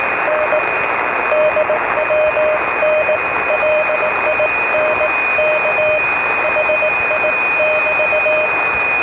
Morse code, heard as a keyed tone, about 18 wpm, 620 Hz, with radio noise. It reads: N DWN LINK HI <BT>